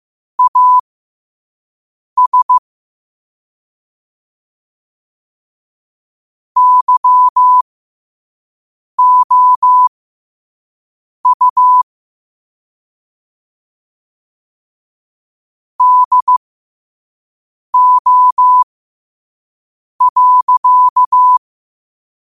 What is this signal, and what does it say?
Morse code, heard as a keyed tone, about 15 wpm, spaced as for 5 wpm, 1000 Hz. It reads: AS YOU DO.
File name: 5wpm